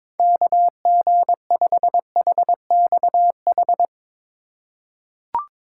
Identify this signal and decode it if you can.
Morse code, heard as a keyed tone, about 21 wpm, 700 Hz. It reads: KG5HXH
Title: USA Call Signs - 6 Characters - 2x Character Spacing - 22wpm